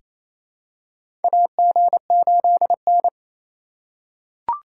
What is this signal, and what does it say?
Morse code, heard as a keyed tone, about 28 wpm, 700 Hz. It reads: AG8N
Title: USA Call Signs - 4 Characters - 3x Character Spacing - 28wpm